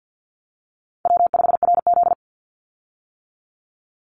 Morse code, heard as a keyed tone, about 50 wpm, 700 Hz. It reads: W5RZ
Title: Easy Call Signs - 50wpm